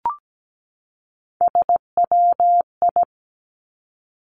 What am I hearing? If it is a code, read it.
Morse code, heard as a keyed tone, about 17 wpm, 700 Hz. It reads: SWI
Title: Three Challenging Letter-Number Combinations 17wpm